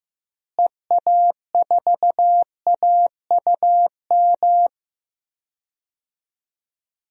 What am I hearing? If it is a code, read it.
Morse code, heard as a keyed tone, about 15 wpm, 700 Hz. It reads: EA4AUM
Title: DXCC-100 Call Signs - 6 Characters - Repeated 3x - 15wpm